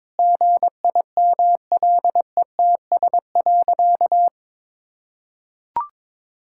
Morse code, heard as a keyed tone, about 22 wpm, 700 Hz. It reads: GIMLETS.
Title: Pangram Challenge v2 - 22wpm